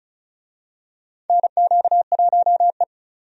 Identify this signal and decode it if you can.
Morse code, heard as a keyed tone, about 36 wpm, 700 Hz. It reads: NQ1E
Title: USA Call Signs - 4 Characters - 35wpm